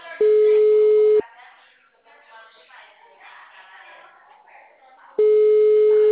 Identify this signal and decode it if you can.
Ringback tone (Europe, 425 Hz), with speech.
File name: out- -2003- - - .73555